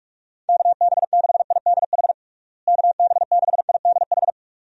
Morse code, heard as a keyed tone, about 46 wpm, 700 Hz. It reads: KB6IDH KB6IDH
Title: USA Call Signs - 6 Characters - Repeated 2x - 45wpm